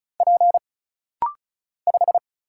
Morse code, heard as a keyed tone, about 35 wpm, 700 Hz. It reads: P 5
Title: Single Letter-Number - Rapid-Fire 35wpm